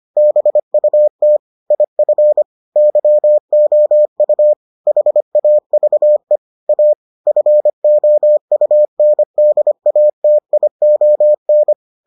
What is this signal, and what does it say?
Morse code, heard as a keyed tone, about 25 wpm, 600 Hz. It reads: BUT IF YOU HAVE A FOUNDATION